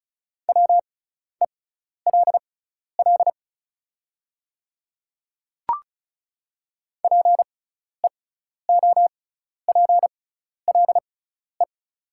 Morse code, heard as a keyed tone, about 36 wpm, spaced as for 12 wpm, 690 Hz. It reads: WELL PEOPLE
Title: 6x Character Spacing - Top 300 Words - 35wpm